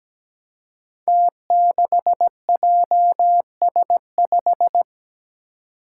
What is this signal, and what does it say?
Morse code, heard as a keyed tone, about 17 wpm, 700 Hz. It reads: T6JS5
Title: Five Challenging Letter-Number Combinations 17wpm